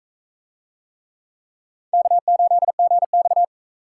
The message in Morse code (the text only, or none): K8GX